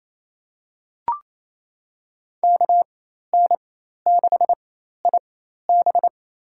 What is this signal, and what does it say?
Morse code, heard as a keyed tone, about 28 wpm, spaced as for 13 wpm, 700 Hz. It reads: KN6IB
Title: USA Call Signs - 5 Characters - 4x Character Spacing - 28wpm